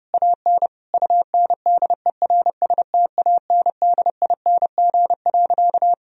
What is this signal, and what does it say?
Morse code, heard as a keyed tone, about 32 wpm, 700 Hz. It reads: AN UNDERSTANDING.
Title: Sentences from Top 1000 Words - Review Part 3 - 30wpm